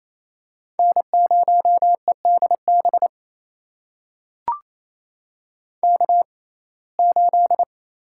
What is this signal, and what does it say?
Morse code, heard as a keyed tone, about 28 wpm, spaced as for 20 wpm, 700 Hz. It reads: N0EDB K8
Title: USA Call Signs - 5 Characters - 6x Character Spacing - 28wpm